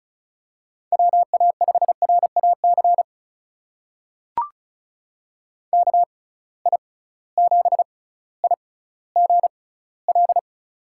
Morse code, heard as a keyed tone, about 36 wpm, spaced as for 18 wpm, 700 Hz. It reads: WA5RAC KI7IGL
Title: USA Call Signs - 6 Characters - 6x Character Spacing - 35wpm